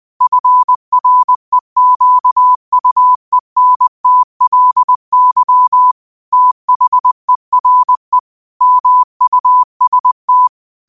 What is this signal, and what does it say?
Morse code, heard as a keyed tone, about 20 wpm, 1000 Hz. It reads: FREQUENTLY THERE MUST